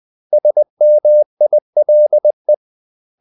Morse code, heard as a keyed tone, about 20 wpm, 600 Hz. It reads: SMILE
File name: CW / Practice / 500MostCommonEnglishWords20